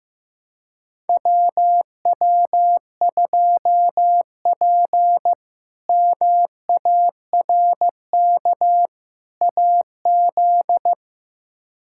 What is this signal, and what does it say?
Morse code, heard as a keyed tone, about 15 wpm, 700 Hz. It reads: WW2P MARK AZ
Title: K1USN SST QSO Practice - 15wpm